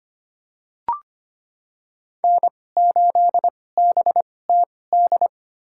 Morse code, heard as a keyed tone, about 25 wpm, spaced as for 18 wpm, 700 Hz. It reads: N8BTD
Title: USA Call Signs - 5 Characters - 2x Character Spacing - 25wpm